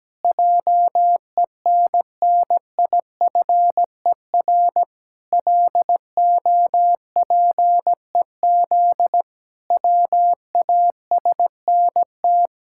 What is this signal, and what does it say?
Morse code, heard as a keyed tone, about 17 wpm, 700 Hz. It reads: JENNIFER LOPEZ WASNT
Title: Pangram Challenge v2 - 17wpm